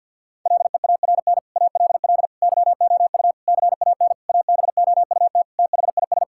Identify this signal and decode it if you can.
Morse code, heard as a keyed tone, about 52 wpm, 700 Hz. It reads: LEARN ALL YOU CAN ABOUT THIS
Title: Sentences from Top 1000 Words - Review Part 5 - 50wpm